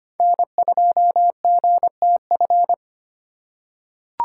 Morse code, heard as a keyed tone, about 25 wpm, 700 Hz. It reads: N2GTF
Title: USA Call Signs - 5 Characters - 2x Character Spacing - 25wpm